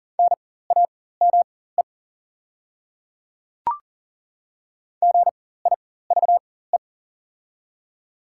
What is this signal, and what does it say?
Morse code, heard as a keyed tone, about 40 wpm, spaced as for 19 wpm, 700 Hz. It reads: NAME GIVE
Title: 4x Character Spacing - Top 200 Words - 40wpm